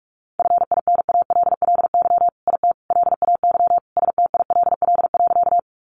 Morse code, heard as a keyed tone, about 44 wpm, 700 Hz. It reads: FINALLY IT LAY STILL.
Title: Sentences from Top 1000 Words 45wpm